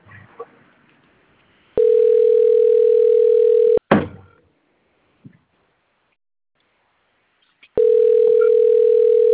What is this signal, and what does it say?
Ringback tone (North America), 440 + 480 Hz.